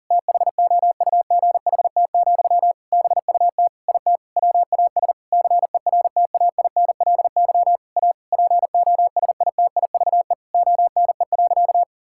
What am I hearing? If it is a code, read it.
Morse code, heard as a keyed tone, about 40 wpm, 700 Hz. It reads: THOUGHT, BUT IT WAS CERTAINLY A POSITIVE ONE.
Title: Sentences from Top 1000 Words - Review Part 6 - 40wpm